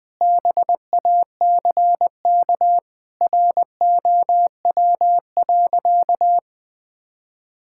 Morse code, heard as a keyed tone, about 20 wpm, 700 Hz. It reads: BACK ROW.
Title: Sentences with 3-Word Phrases - 20wpm